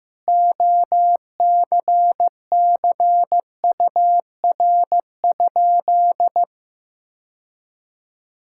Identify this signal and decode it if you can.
Morse code, heard as a keyed tone, about 15 wpm, 700 Hz. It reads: OCCUR?
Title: Sentences from Top 1000 Words - Review Part 6 - 15wpm